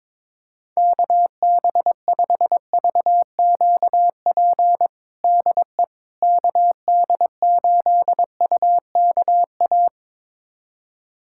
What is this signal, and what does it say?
Morse code, heard as a keyed tone, about 21 wpm, 700 Hz. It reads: KB5VQP DE KD8UKA
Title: USA Callsign DE Callsign - 6 Characters - 22wpm